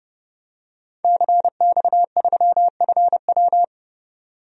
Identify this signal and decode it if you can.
Morse code, heard as a keyed tone, about 30 wpm, 700 Hz. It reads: CX3FW
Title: DXCC-100 Call Signs - 5 Characters - 30wpm